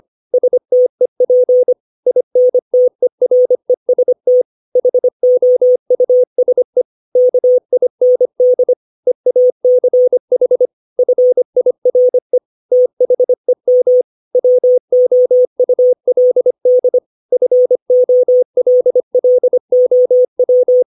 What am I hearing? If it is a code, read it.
Morse code, heard as a keyed tone, about 25 wpm, 500 Hz. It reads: STEP INTEREST HOUSE KIND EACH FIRE THEM WOULD FOLLOW
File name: 632 words / 632 Words 25WPM